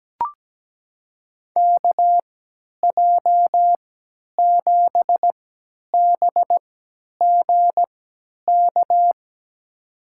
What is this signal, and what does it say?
Morse code, heard as a keyed tone, about 17 wpm, spaced as for 10 wpm, 700 Hz. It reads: KJ7BGK
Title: USA Call Signs - 6 Characters - 3x Character Spacing - 17wpm